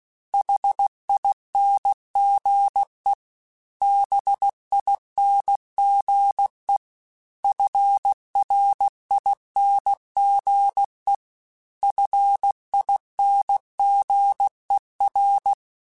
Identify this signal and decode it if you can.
Morse code, heard as a keyed tone, about 16 wpm, 800 Hz. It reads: HINGE BINGE FRINGE FINGER
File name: Special INGs-mp3s / Special INGs-mp3s / Hidden ings @15